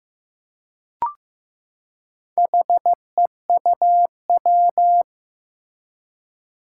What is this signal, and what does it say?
Morse code, heard as a keyed tone, about 15 wpm, 700 Hz. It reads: HEUW
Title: Four Challenging Letter-Number Combinations 15wpm